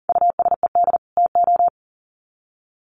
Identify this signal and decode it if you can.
Morse code, heard as a keyed tone, about 40 wpm, 700 Hz. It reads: USED TO
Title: Sets of 2 Words 40wpm